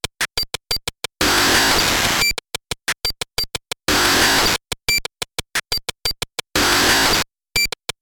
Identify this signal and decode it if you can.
There is no code.